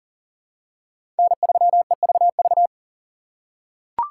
Morse code, heard as a keyed tone, about 43 wpm, 690 Hz. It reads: N3EVV
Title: USA Call Signs - 5 Characters - 6x Character Spacing - 40wpm